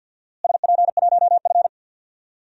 Morse code, heard as a keyed tone, about 47 wpm, 700 Hz. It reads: SP1F
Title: DXCC-100 Call Signs - 4 Characters - Repeated 2x - 50wpm